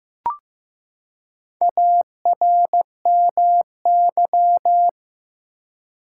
Morse code, heard as a keyed tone, about 15 wpm, 700 Hz. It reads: ARMY
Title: Four Letter Words - 15wpm